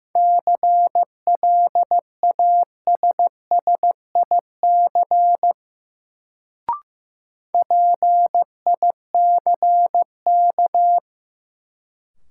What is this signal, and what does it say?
Morse code, heard as a keyed tone, about 15 wpm, 700 Hz. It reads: CLASSIC PICK
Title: Sentences from Top 1000 Words - New Word Drill - 15wpm